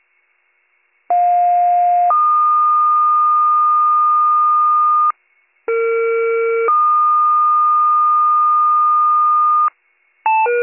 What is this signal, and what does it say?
Two-tone page, heard as a one-second tone, then a three-second tone, about 710 then 1220 Hz, about 460 then 1220 Hz. It is heard from 1.1 to 9.7 s.